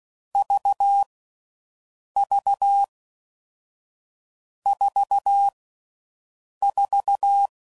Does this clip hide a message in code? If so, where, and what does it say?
Yes: Morse code from 0.3 to 7.5 s, heard as a keyed tone, about 16 wpm, 800 Hz. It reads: V V 4 4